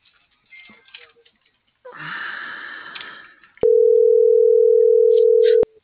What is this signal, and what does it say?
Ringback tone (North America), 440 + 480 Hz.